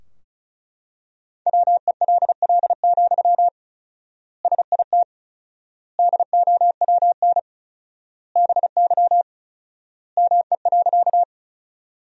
Morse code, heard as a keyed tone, about 36 wpm, 700 Hz. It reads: WELL, SIT DOWN BY ME.
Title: 4x Word-Spacing - Sentences from Top 500 Words - 35wpm